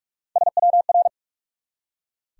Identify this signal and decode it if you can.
Morse code, heard as a keyed tone, about 46 wpm, 700 Hz. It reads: SWR
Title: Top 180 QSO Elements 45wpm